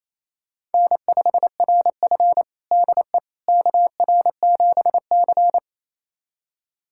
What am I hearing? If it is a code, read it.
Morse code, heard as a keyed tone, about 28 wpm, 700 Hz. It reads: N5RF DE KR7C
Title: USA Callsign DE Callsign - 4 Characters - 28wpm